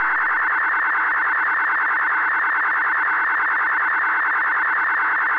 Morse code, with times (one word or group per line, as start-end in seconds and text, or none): none